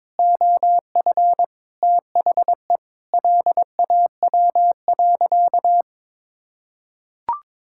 Morse code, heard as a keyed tone, about 22 wpm, 700 Hz. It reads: OF THE LAW.
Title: Sentences from Top 1600 Words 22wpm